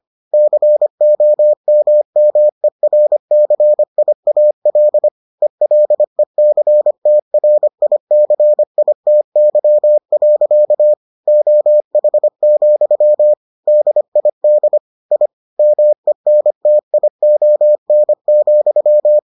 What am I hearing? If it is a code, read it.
Morse code, heard as a keyed tone, about 25 wpm, 600 Hz. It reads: COMMERCIAL ELECTRICITY. OH, DID I MENTION,